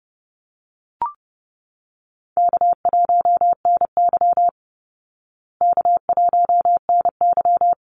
Morse code, heard as a keyed tone, about 30 wpm, 700 Hz. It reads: K1NY K1NY
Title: Easy Call Signs Repeated 2x - 30wpm